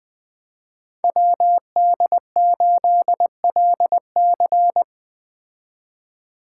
Morse code, heard as a keyed tone, about 21 wpm, 700 Hz. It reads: WD8LC
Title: Speed Racing - USA 2x2 Call Signs 17wpm